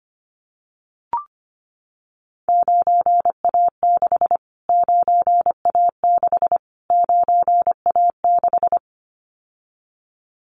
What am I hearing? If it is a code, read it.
Morse code, heard as a keyed tone, about 25 wpm, 700 Hz. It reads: 9A6 9A6 9A6
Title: Callsign Prefixes for Top 25 DXCC - Repeated 3x - 25wpm